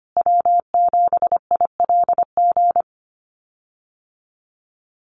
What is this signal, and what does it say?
Morse code, heard as a keyed tone, about 25 wpm, 700 Hz. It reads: W7ILG
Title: USA Call Signs Repeated 3x 25wpm